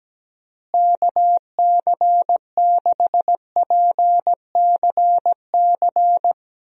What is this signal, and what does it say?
Morse code, heard as a keyed tone, about 17 wpm, 700 Hz. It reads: KC6PCC
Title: USA Call Signs - 6 Characters - Repeated 3x - 17wpm